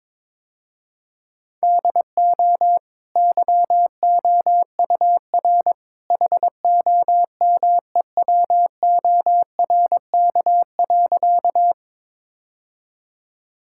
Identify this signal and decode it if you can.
Morse code, heard as a keyed tone, about 22 wpm, 690 Hz. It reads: DO YOUR HOMEWORK.
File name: practice / 22wpm / Easy Sentences with Hints - 22wpm